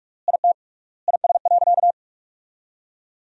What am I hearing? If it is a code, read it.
Morse code, heard as a keyed tone, about 46 wpm, 700 Hz. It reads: IT IS.
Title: 3x Word-Spacing - Sentences from Top 100 Words - 45wpm